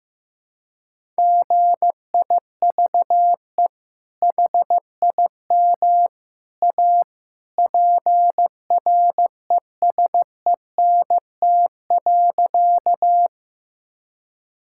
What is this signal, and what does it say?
Morse code, heard as a keyed tone, about 15 wpm, 700 Hz. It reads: GIVE HIM A PRESENT.
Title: Sentences from Top 1000 Words - Review Part 5 - 15wpm